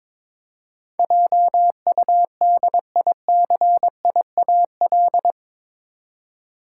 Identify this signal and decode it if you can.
Morse code, heard as a keyed tone, about 22 wpm, 700 Hz. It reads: JUDICIAL
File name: practice / 22wpm / Eight Letter Words - 22wpm